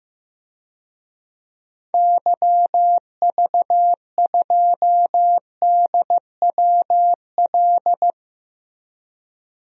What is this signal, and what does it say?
Morse code, heard as a keyed tone, about 15 wpm, 700 Hz. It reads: YV2DWL
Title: DXCC-100 Call Signs - 6 Characters - Repeated 3x - 15wpm